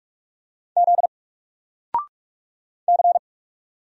Morse code, heard as a keyed tone, about 45 wpm, 700 Hz. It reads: Z C
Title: Single Letter-Number 45wpm